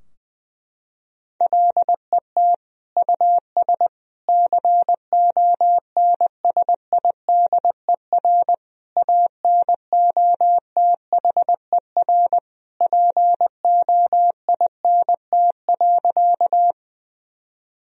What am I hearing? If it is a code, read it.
Morse code, heard as a keyed tone, about 20 wpm, 700 Hz. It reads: LET US CONSIDER ANOTHER POINT.
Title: Sentences from Top 1000 Words - Review Part 4 - 20wpm